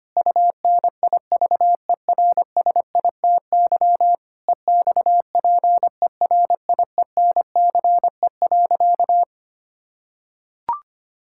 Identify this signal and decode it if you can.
Morse code, heard as a keyed tone, about 25 wpm, 700 Hz. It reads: UNIVERSITY EXPERIENCE.
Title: Sentences from Top 1000 Words - Review Part 5 - 25wpm